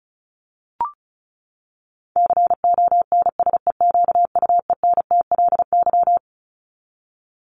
Morse code, heard as a keyed tone, about 34 wpm, 700 Hz. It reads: CONSEQUENTLY
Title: LY Suffix - Top 100 Words with Base Words 35wpm